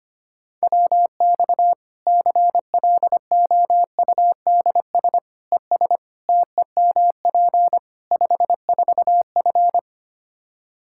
Morse code, heard as a keyed tone, about 25 wpm, 700 Hz. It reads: WX CLOUDS ES TEMP 54F
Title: Practice QSO - Exchange 2 - Info - 25wpm